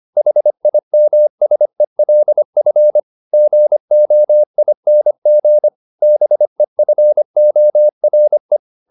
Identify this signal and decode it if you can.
Morse code, heard as a keyed tone, about 25 wpm, 600 Hz. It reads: HIMSELF GOING BEFORE